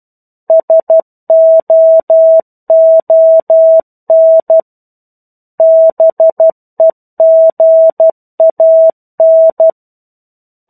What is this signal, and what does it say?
Morse code, heard as a keyed tone, about 12 wpm, 650 Hz. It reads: SOON BEGAN
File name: 01-Chapter-01 12 WPM